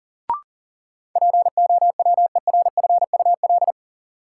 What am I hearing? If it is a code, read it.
Morse code, heard as a keyed tone, about 40 wpm, 700 Hz. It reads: POWERFUL
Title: Sentences from Top 1300 Words - New Word Drill Alt - 40wpm